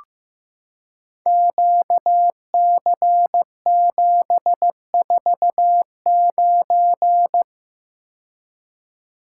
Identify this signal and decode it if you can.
Morse code, heard as a keyed tone, about 15 wpm, 700 Hz. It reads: QC749
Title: Code Groups - 5 Letter:Numbers - 15wpm